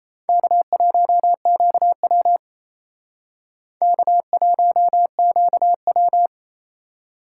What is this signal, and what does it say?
Morse code, heard as a keyed tone, about 35 wpm, 700 Hz. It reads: K1QW K1QW
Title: Speed Racing - USA 1x2 Call Signs 28wpm